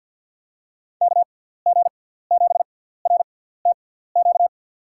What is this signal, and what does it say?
Morse code, heard as a keyed tone, about 52 wpm, spaced as for 17 wpm, 700 Hz. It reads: KG7RTQ
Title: USA Call Signs - 6x Character Spacing - 50wpm